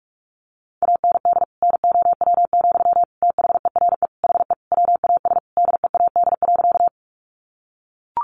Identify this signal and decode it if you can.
Morse code, heard as a keyed tone, about 44 wpm, 700 Hz. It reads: AND NOW, THERE HE WAS DEAD.
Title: Sentences from Top 1500 Words 45wpm